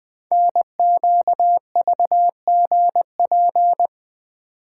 Morse code, heard as a keyed tone, about 20 wpm, 700 Hz. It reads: NQVGP
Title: Code Groups - 5 Letters - 20wpm